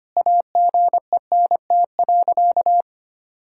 Morse code, heard as a keyed tone, about 25 wpm, 700 Hz. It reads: AGENT.
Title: Sentences from Top 1600 Words 25wpm